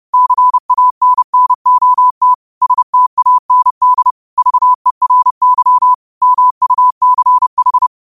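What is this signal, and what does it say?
Morse code, heard as a keyed tone, about 30 wpm, 1000 Hz. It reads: CANNOT STAND VERY MUCH
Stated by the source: Morse code quotes of the day (QOTD) sent at 30 words per minute.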